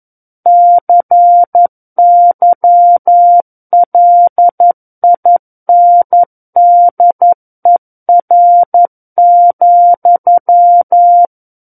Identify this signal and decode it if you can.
Morse code, heard as a keyed tone, about 11 wpm, 700 Hz. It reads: CYLINDER,
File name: Book One / 02-Chapter 2 - 11 WPM